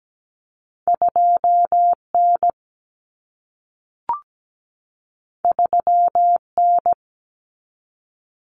Morse code, heard as a keyed tone, about 17 wpm, 700 Hz. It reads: 2N 3N
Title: Common Cut Numbers - 2-Digit 17wpm